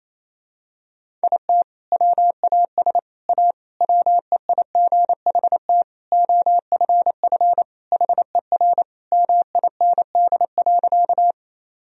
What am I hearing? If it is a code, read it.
Morse code, heard as a keyed tone, about 27 wpm, 700 Hz. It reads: IT WAS A WEIGHT OFF HER MIND.
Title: Sentences from Top 1000 Words - Review Part 3 - 28wpm